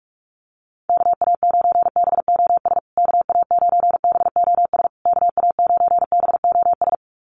Morse code, heard as a keyed tone, about 44 wpm, 690 Hz. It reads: KA9BOS KA9BOS KA9BOS
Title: USA Call Signs Repeated 3x 45wpm